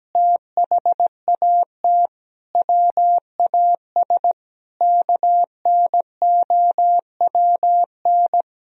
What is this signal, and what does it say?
Morse code, heard as a keyed tone, about 17 wpm, 700 Hz. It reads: THAT WAS KNOWN
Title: Sentences from Top 1000 Words - Review Part 3 - 17wpm